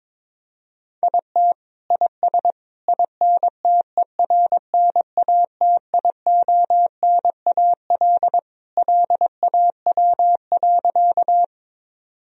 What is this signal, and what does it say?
Morse code, heard as a keyed tone, about 22 wpm, 700 Hz. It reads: IT IS INTERNATIONAL LAW.
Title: Sentences from Top 1000 Words - Review Part 4 - 22wpm